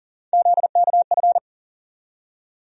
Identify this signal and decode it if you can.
Morse code, heard as a keyed tone, about 40 wpm, 700 Hz. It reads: ZKF
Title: USA Call Sign Postfixes - 3 Letters 40wpm